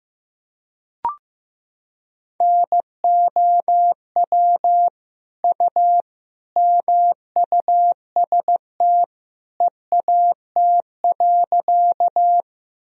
Morse code, heard as a keyed tone, about 15 wpm, 700 Hz. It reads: NOW U MUST EAT.
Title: Sentences from Top 300 Words and Abbreviations 15wpm